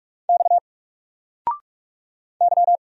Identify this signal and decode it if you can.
Morse code, heard as a keyed tone, about 45 wpm, 700 Hz. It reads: X Y
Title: Single Letter-Number Alt 45wpm